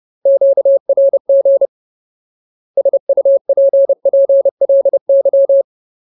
Morse code, heard as a keyed tone, about 30 wpm, 550 Hz. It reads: QRG SUPPLY
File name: Ultimate Word List / 30WPM Ultimate